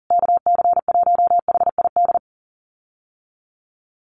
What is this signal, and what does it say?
Morse code, heard as a keyed tone, about 40 wpm, 700 Hz. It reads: KC1HID
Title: USA Call Signs Repeated 3x 40wpm